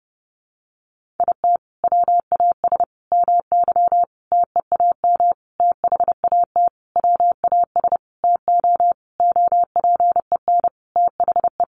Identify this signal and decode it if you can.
Morse code, heard as a keyed tone, about 30 wpm, 700 Hz. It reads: IT WAS MY TEAM THAT WAS TO OPEN THE
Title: Sentences from Top 1500 Words 30wpm